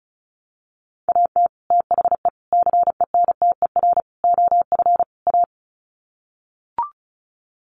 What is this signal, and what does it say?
Morse code, heard as a keyed tone, about 34 wpm, 700 Hz. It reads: AT THE CENTER OF A
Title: Sets of 5 Words Encore2 35wpm